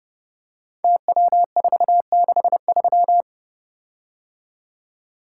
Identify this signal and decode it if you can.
Morse code, heard as a keyed tone, about 29 wpm, 700 Hz. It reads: TW463
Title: Code Groups - 5 Letter:Numbers - 30wpm